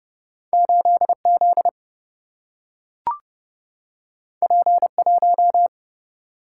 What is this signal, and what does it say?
Morse code, heard as a keyed tone, about 30 wpm, 700 Hz. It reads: 8 Z P 1
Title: Single-Letter Single-Number 30wpm